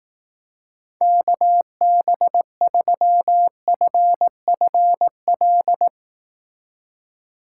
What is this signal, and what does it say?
Morse code, heard as a keyed tone, about 19 wpm, 700 Hz. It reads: KB3FFL
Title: Speed Racing - USA 2x3 Call Signs 15wpm